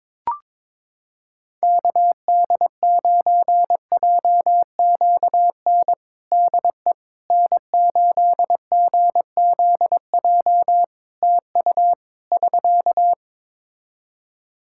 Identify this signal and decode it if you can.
Morse code, heard as a keyed tone, about 22 wpm, 700 Hz. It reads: KD9JQN DE N8GZJ TU <SK>
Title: K1USN SST QSO Practice 22wpm